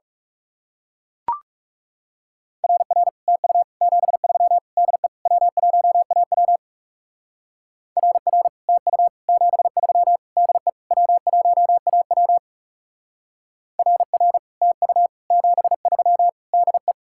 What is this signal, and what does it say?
Morse code, heard as a keyed tone, about 36 wpm, 700 Hz. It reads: RR TU 73 DE W1AW RR TU 73 DE W1AW RR TU 73 DE
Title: Speed-Racing Common QSO Phrases Encore - 30wpm